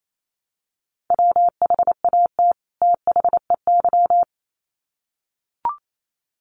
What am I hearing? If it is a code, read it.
Morse code, heard as a keyed tone, about 28 wpm, 700 Hz. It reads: WHAT THEY
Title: Sets of 2 Words 28wpm